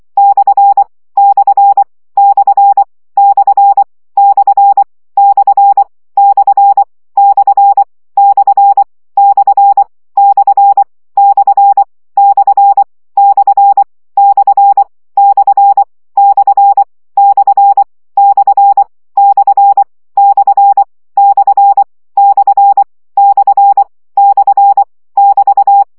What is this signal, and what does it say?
Morse code, heard as a keyed tone, about 24 wpm, 800 Hz. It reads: / / / / / / / / / / / / / / / / / / / / / / / / / <BT>
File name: CW gyakorlas per 20 baud